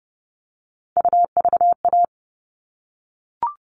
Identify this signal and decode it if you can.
Morse code, heard as a keyed tone, about 30 wpm, 700 Hz. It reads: UVA
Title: All Cut Numbers - 3-Digit 30wpm